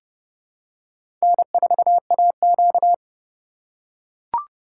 Morse code, heard as a keyed tone, about 30 wpm, 700 Hz. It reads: N4AQ
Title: USA Call Signs - 4 Characters - 3x Character Spacing - 30wpm